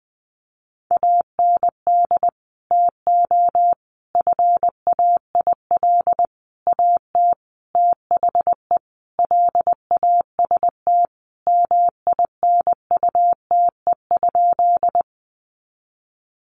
Morse code, heard as a keyed tone, about 20 wpm, 700 Hz. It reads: AND TO FAIL AT THE LAST MINUTE?
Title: Sentences from Top 1100 Words 20wpm